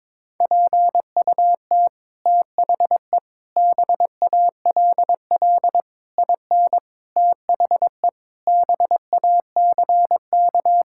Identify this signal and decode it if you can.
Morse code, heard as a keyed tone, about 22 wpm, 700 Hz. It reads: PUT THE BALL IN THE BACK
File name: Sets of 9 Words - 22wpm